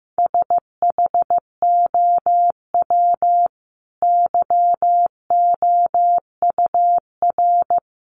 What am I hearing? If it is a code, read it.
Morse code, heard as a keyed tone, about 15 wpm, 700 Hz. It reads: SHOW YOUR
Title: Sentences from Top 1400 Words 15wpm